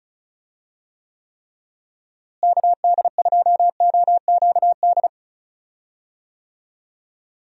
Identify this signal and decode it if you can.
Morse code, heard as a keyed tone, about 34 wpm, 700 Hz. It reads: KD2OQD
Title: USA Call Signs - 6 Characters - Repeated 3x - 35wpm